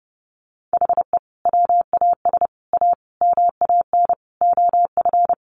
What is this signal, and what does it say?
Morse code, heard as a keyed tone, about 30 wpm, 700 Hz. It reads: HE WAS A MAN OF
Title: Sentences from Top 1200 Words 30wpm